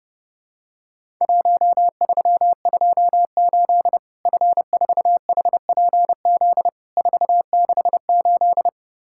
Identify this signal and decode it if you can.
Morse code, heard as a keyed tone, about 30 wpm, 700 Hz. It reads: 1328 F4HPZ 468
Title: Worked All Europe DX Contest - QTC Exchange - 30wpm